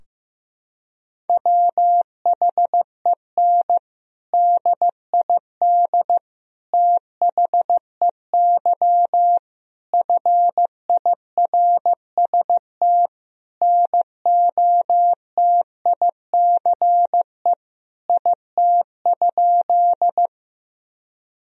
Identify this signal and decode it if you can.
Morse code, heard as a keyed tone, about 15 wpm, 700 Hz. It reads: WHEN DID THEY FIRST NOTICE IT?
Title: Sentences from Top 1000 Words - Review Part 2 - 15wpm